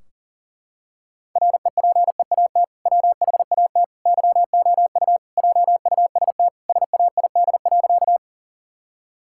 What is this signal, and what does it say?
Morse code, heard as a keyed tone, about 40 wpm, 700 Hz. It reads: REPEAT WHAT YOU JUST SAID.
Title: Sentences from Top 1600 Words 40wpm